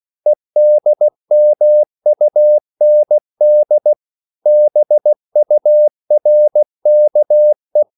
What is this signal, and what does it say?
Morse code, heard as a keyed tone, about 16 wpm, 600 Hz. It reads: EDMUND BURKE
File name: CW / Practice / EnglishQuotations0416